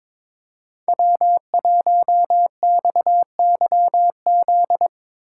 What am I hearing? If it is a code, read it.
Morse code, heard as a keyed tone, about 22 wpm, 700 Hz. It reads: W1XYZ
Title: USA Call Signs - 5 Characters - 2x Character Spacing - 22wpm